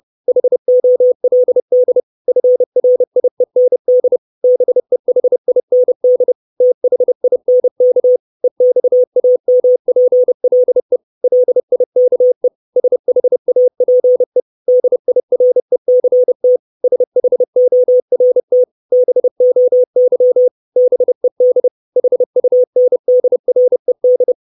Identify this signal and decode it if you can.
Morse code, heard as a keyed tone, about 30 wpm, 500 Hz. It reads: HOLD FRIEND BEHIND THINK EXAMPLE LIKE SHAPE DIRECT SHORT BOY BED HUNDRED
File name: 632 Words 30WPM